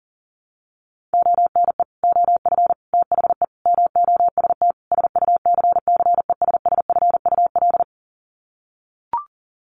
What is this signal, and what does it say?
Morse code, heard as a keyed tone, about 40 wpm, 700 Hz. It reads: ONE OF THE MOST SUCCESSFUL
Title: Sets of 5 Words Encore2 40wpm